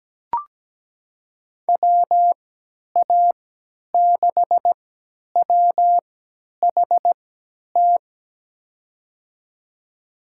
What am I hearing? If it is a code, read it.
Morse code, heard as a keyed tone, about 17 wpm, spaced as for 10 wpm, 700 Hz. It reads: WA6WHT
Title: USA Call Signs - 6 Characters - 3x Character Spacing - 17wpm